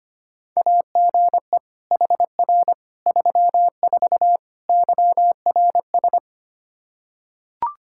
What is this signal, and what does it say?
Morse code, heard as a keyed tone, about 25 wpm, 700 Hz. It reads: AGE HR 34 YRS
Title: K1USN SST QSO Practice 25wpm